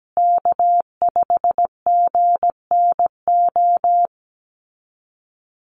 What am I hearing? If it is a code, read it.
Morse code, heard as a keyed tone, about 17 wpm, 700 Hz. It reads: K5GNO
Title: USA Call Signs Repeated 2x 17wpm